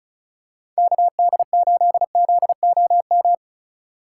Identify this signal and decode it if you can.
Morse code, heard as a keyed tone, about 34 wpm, 700 Hz. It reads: KD8ZOM
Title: USA Call Signs - 6 Characters - 2x Character Spacing - 35wpm